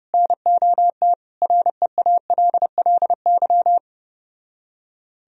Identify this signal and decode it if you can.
Morse code, heard as a keyed tone, about 31 wpm, 700 Hz. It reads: NOT REALLY